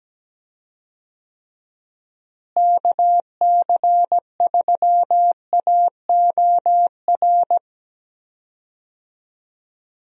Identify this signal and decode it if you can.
Morse code, heard as a keyed tone, about 17 wpm, 700 Hz. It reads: KC3AOR
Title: USA Call Signs - 6 Characters - Repeated 2x - 17wpm